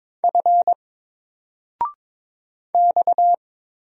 Morse code, heard as a keyed tone, about 22 wpm, 700 Hz. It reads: F X
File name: Single-Letter-Number-22wpm